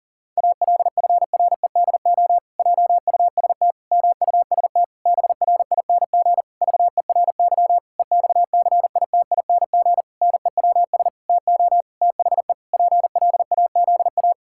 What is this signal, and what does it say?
Morse code, heard as a keyed tone, about 40 wpm, 700 Hz. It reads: ALFREDO JUST MUST BRING VERY EXCITING NEWS TO THE PLAZA
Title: Pangram Challenge v2 - 40wpm